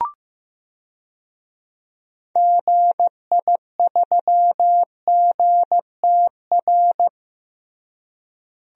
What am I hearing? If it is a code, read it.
Morse code, heard as a keyed tone, about 15 wpm, 700 Hz. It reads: GI3GTR
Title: DXCC-100 Call Signs - 6 Characters - Repeated 2x - 15wpm